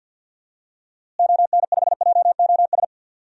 Morse code, heard as a keyed tone, about 45 wpm, 700 Hz. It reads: ON5JOS
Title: DXCC-100 Call Signs - 6 Characters - Repeated 2x - 50wpm